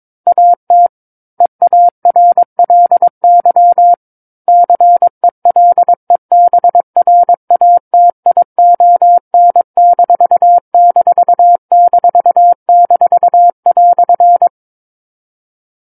Morse code, heard as a keyed tone, about 22 wpm, 700 Hz. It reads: AT EARLY CELEBRATION----"
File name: Book One / 13-Chapter 13 - 22 WPM